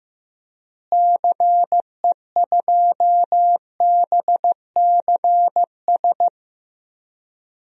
Morse code, heard as a keyed tone, about 15 wpm, 700 Hz. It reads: CE2BCS
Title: DXCC-100 Call Signs - 6 Characters - Repeated 2x - 15wpm